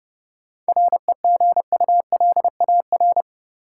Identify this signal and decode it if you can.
Morse code, heard as a keyed tone, about 30 wpm, 700 Hz. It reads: REGULAR
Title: Sentences from Top 900 Words - New Word Drill - 30wpm